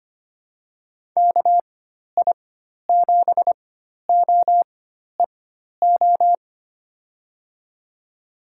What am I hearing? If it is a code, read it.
Morse code, heard as a keyed tone, about 25 wpm, spaced as for 12 wpm, 700 Hz. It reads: KI7OEO
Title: USA Call Signs - 6 Characters - 4x Character Spacing - 25wpm